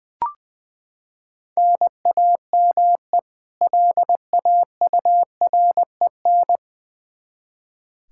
Morse code, heard as a keyed tone, about 21 wpm, 700 Hz. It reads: NAME LAUREN
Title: K1USN SST QSO Practice 20wpm